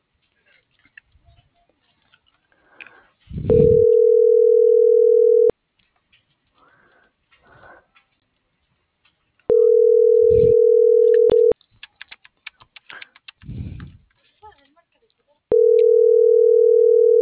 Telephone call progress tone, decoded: ringback tone

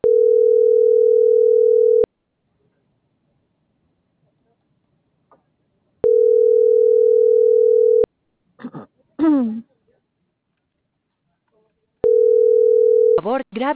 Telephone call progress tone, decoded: ringback tone

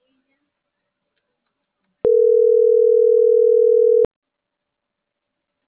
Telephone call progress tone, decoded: ringback tone

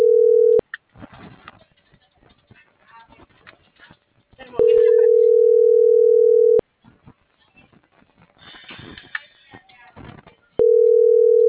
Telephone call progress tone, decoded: ringback tone

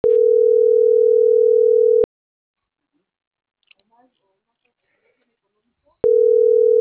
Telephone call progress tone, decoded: ringback tone

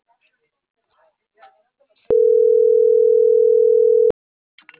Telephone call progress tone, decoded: ringback tone

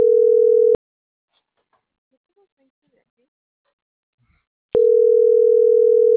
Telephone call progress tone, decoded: ringback tone

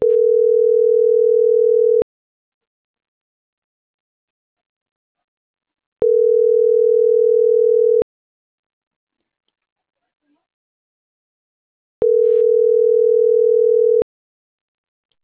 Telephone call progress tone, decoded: ringback tone

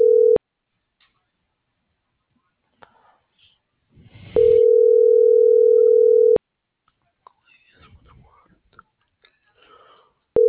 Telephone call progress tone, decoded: ringback tone

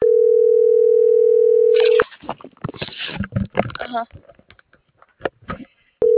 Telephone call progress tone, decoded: ringback tone

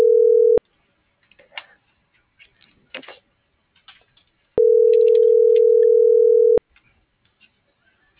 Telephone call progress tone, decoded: ringback tone

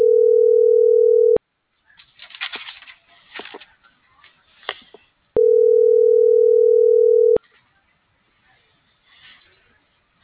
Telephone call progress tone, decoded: ringback tone

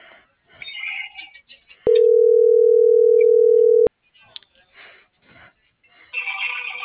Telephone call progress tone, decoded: ringback tone

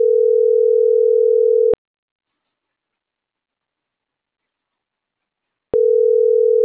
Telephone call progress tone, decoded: ringback tone

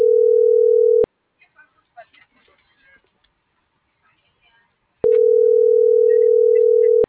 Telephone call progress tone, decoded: ringback tone

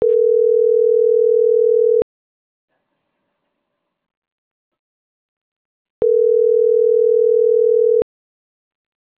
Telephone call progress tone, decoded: ringback tone

